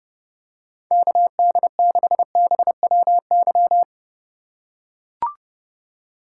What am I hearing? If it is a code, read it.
Morse code, heard as a keyed tone, about 30 wpm, 700 Hz. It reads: KD6BWY
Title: USA Call Signs - 6 Characters - 3x Character Spacing - 30wpm